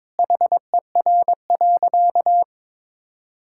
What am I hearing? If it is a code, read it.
Morse code, heard as a keyed tone, about 22 wpm, 700 Hz. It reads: HER.
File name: practice / 22wpm / Sentences with 4-Word Phrases - 22wpm